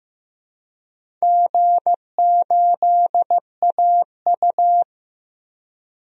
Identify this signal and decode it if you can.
Morse code, heard as a keyed tone, about 15 wpm, 700 Hz. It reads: G8AU
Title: DXCC-100 Call Signs - 4 Characters - 15wpm